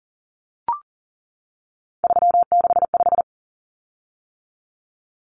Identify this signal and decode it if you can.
Morse code, heard as a keyed tone, about 40 wpm, 700 Hz. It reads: 365
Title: Common Cut Numbers - 3-Digit 40wpm